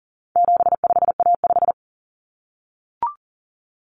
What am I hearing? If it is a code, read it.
Morse code, heard as a keyed tone, about 40 wpm, 700 Hz. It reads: 75A5
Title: Common Cut Numbers - 4-Digit 40wpm